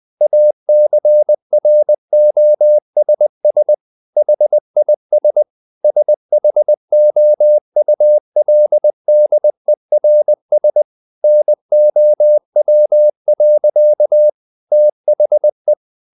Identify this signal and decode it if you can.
Morse code, heard as a keyed tone, about 20 wpm, 600 Hz. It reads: ACROSS HIS SHOULDERS NOW. THE
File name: the old man and the sea / the old man and the sea part 1 20wpm